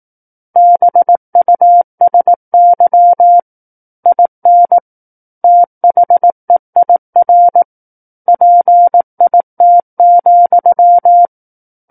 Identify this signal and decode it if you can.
Morse code, heard as a keyed tone, about 18 wpm, 700 Hz. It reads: BUSY IN THEIR PIT,
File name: Book One / 09-Chapter 9 - 18 WPM